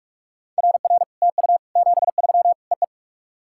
Morse code, heard as a keyed tone, about 44 wpm, 700 Hz. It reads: RR TU 73 EE
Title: SOTA QSO Practice 45wpm